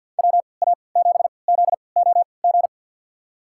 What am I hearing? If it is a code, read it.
Morse code, heard as a keyed tone, about 50 wpm, spaced as for 28 wpm, 700 Hz. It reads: WA7ZOG
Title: USA Call Signs - 6 Characters - 3x Character Spacing - 50wpm